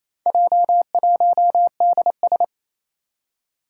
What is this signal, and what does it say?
Morse code, heard as a keyed tone, about 27 wpm, 700 Hz. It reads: J1DS
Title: Four Challenging Letter-Number Combinations 28wpm